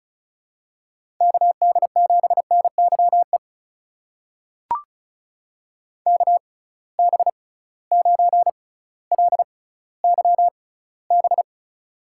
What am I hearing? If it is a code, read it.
Morse code, heard as a keyed tone, about 36 wpm, spaced as for 18 wpm, 700 Hz. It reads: KD7NYE KB9LYB
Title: USA Call Signs - 6 Characters - 6x Character Spacing - 35wpm